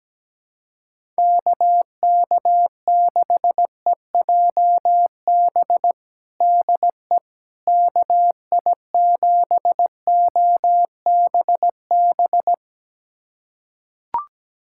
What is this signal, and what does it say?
Morse code, heard as a keyed tone, about 17 wpm, 700 Hz. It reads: KK6EJB DE KI7OBB
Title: USA Callsign DE Callsign - 6 Characters - 17wpm